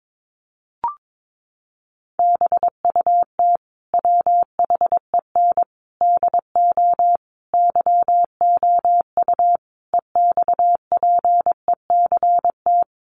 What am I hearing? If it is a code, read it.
Morse code, heard as a keyed tone, about 22 wpm, 700 Hz. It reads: BUT WHEN DO YOU EXPECT
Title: Sentences from Top 1100 Words 22wpm